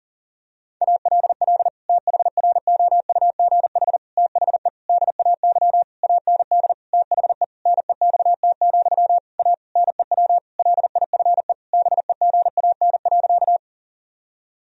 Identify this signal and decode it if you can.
Morse code, heard as a keyed tone, about 40 wpm, 700 Hz. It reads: ALL THROUGH THE DAY AND THE NEXT, A NEW LIFE BEGAN.
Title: Sentences from Top 1000 Words - Review Part 1 - 40wpm